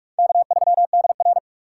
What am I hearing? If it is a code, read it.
Morse code, heard as a keyed tone, about 56 wpm, 700 Hz. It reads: K3DR
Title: Speed Racing - USA 1x2 Call Signs 45wpm